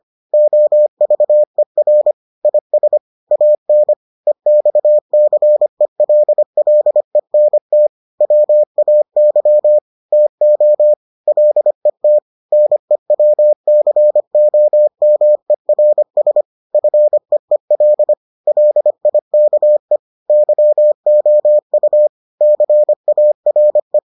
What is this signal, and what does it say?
Morse code, heard as a keyed tone, about 25 wpm, 600 Hz. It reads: OVER IS AN EXCELLENT WAY TO LET NEWCOMERS FEEL LIKE YOU CARE